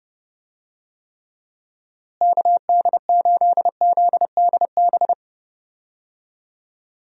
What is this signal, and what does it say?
Morse code, heard as a keyed tone, about 30 wpm, 700 Hz. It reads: KD8ZDB
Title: USA Call Signs - 6 Characters - Repeated 3x - 30wpm